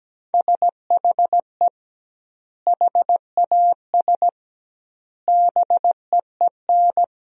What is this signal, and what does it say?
Morse code, heard as a keyed tone, about 17 wpm, 700 Hz. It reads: SHE HAS BEEN
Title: 2x Word-Spacing - Sentences from Top 200 Words - 17wpm